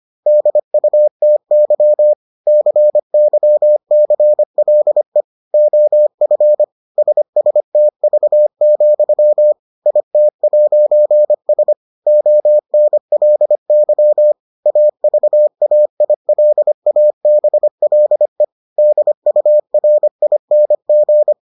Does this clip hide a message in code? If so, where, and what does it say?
Yes: Morse code from 0.3 to 21.3 s, heard as a keyed tone, about 25 wpm, 600 Hz. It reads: DUTY CYCLE OF SSTV, IT'S ONLY AVAILABLE DURING